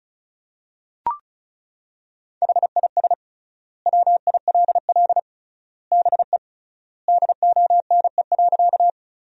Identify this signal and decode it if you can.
Morse code, heard as a keyed tone, about 36 wpm, 700 Hz. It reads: HIS WILL BE DONE.
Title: 3x Word-Spacing - Sentences from Top 500 Words - 35wpm